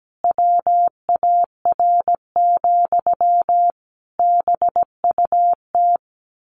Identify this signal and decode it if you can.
Morse code, heard as a keyed tone, about 17 wpm, 700 Hz. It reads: WAR, BUT
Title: Sentences from Top 1200 Words 17wpm